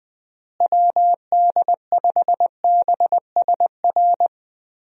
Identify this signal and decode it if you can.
Morse code, heard as a keyed tone, about 20 wpm, 700 Hz. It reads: WD5BSR
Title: USA Call Signs - 6 Characters - Repeated 3x - 20wpm